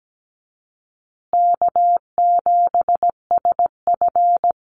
Morse code, heard as a keyed tone, about 17 wpm, 700 Hz. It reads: K7SF
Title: Easy Call Signs Repeated 3x - 17wpm